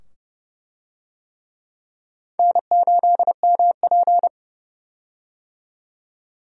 Morse code, heard as a keyed tone, about 30 wpm, 700 Hz. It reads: N8MP
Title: USA Call Signs - 4 Characters - 30wpm